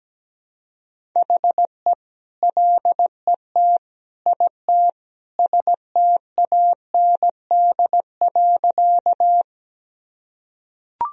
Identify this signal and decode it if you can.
Morse code, heard as a keyed tone, about 17 wpm, 700 Hz. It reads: HE LET IT STAND.
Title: Sentences from Top 1000 Words - Review Part 1 - 17wpm